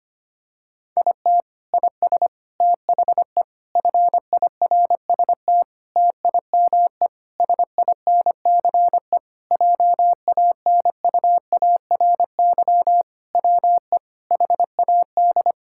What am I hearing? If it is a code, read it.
Morse code, heard as a keyed tone, about 25 wpm, 700 Hz. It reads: IT IS THE FIRST TIME SINCE JANUARY WE HAD
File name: practice / 25wpm / Sentences with 4-Word Phrases - 25wpm